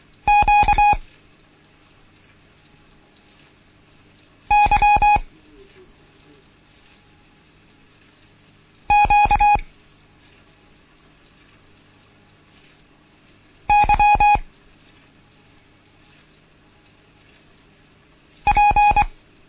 Morse code, heard as a keyed tone, about 24 wpm, 830 Hz. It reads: Q Y Q Y P
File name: Morsen / Kurs A 16